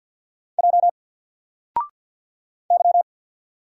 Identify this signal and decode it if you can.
Morse code, heard as a keyed tone, about 48 wpm, 700 Hz. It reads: J Y
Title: Single Letter-Number 50wpm